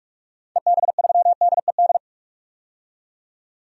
Morse code, heard as a keyed tone, about 46 wpm, 700 Hz. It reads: EB3DED
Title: DXCC-100 Call Signs - 6 Characters - 45wpm